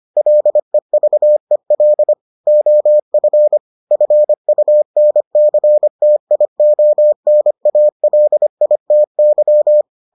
Morse code, heard as a keyed tone, about 25 wpm, 600 Hz. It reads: LEVEL OF FUNCTIONALITY